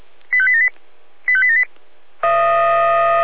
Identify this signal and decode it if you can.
Five-tone selective call (ZVEI), heard as quick runs of five tones, tones of about 70 ms. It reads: 85878 85878